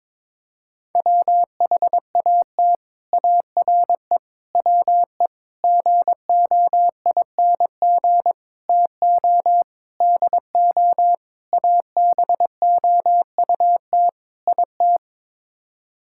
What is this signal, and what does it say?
Morse code, heard as a keyed tone, about 22 wpm, 700 Hz. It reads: WHAT ARE WE GOING TO DO ABOUT IT
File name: practice / 22wpm / Sets of 8 Words - 22wpm